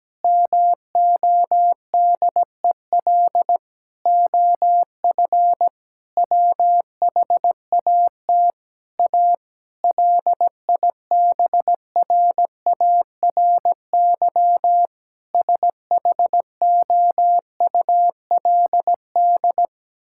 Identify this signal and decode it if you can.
Morse code, heard as a keyed tone, about 17 wpm, 700 Hz. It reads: MODEL OF WHAT A LIBRARY SHOULD
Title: Sentences from Top 1000 Words - Review Part 5 - 17wpm